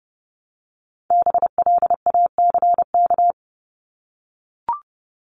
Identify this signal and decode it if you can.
Morse code, heard as a keyed tone, about 30 wpm, 700 Hz. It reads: BLACK
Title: Top 500 Words 30wpm